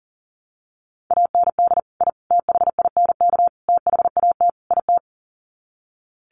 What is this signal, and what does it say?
Morse code, heard as a keyed tone, about 40 wpm, 700 Hz. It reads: AND I THINK THAT IT
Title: Sets of 5 Words Encore2 40wpm